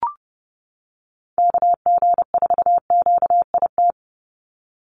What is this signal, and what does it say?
Morse code, heard as a keyed tone, about 30 wpm, 700 Hz. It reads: KG4QIT
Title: USA Call Signs 30wpm